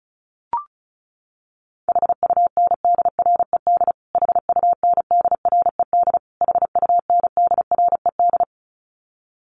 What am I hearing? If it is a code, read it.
Morse code, heard as a keyed tone, about 34 wpm, 700 Hz. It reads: HUNDRED HUNDRED HUNDRED
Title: Top 500 Words Repeated 35wpm